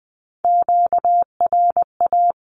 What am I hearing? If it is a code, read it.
Morse code, heard as a keyed tone, about 20 wpm, 700 Hz. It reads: QRA
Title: Q-Codes 20wpm